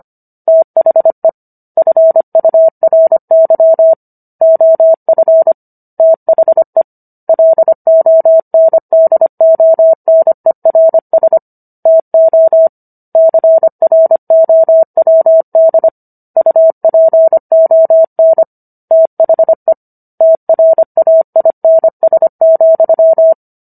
Morse code, heard as a keyed tone, about 25 wpm, 650 Hz. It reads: THE FURY OF THE LONDONERS TO CROWD UPON THE TRAINS,